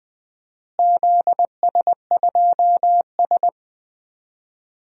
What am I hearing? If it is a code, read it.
Morse code, heard as a keyed tone, about 20 wpm, 700 Hz. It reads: ZS2S
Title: DXCC-100 Call Signs - 4 Characters - Repeated 2x - 20wpm